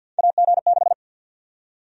Morse code, heard as a keyed tone, about 51 wpm, 700 Hz. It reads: AG6
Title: USA Call Sign Prefixes - 50wpm